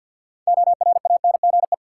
Morse code, heard as a keyed tone, about 51 wpm, 700 Hz. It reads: ORANGE
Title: Six Letter Words - 50wpm